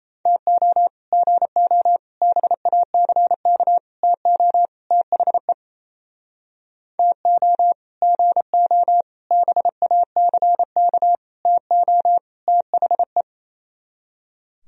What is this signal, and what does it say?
Morse code, heard as a keyed tone, about 36 wpm, 700 Hz. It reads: TO GO BACK TO THE TO GO BACK TO THE